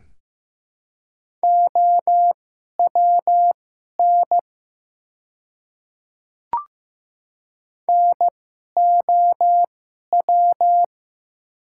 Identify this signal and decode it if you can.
Morse code, heard as a keyed tone, about 15 wpm, spaced as for 11 wpm, 700 Hz. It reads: OWN NOW
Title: 2x Character Spacing - Top 300 Words - 15wpm